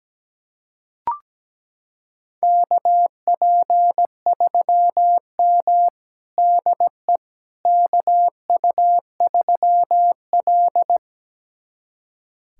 Morse code, heard as a keyed tone, about 17 wpm, 700 Hz. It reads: KP3M DE KU3L
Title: USA Callsign DE Callsign - 4 Characters - 17wpm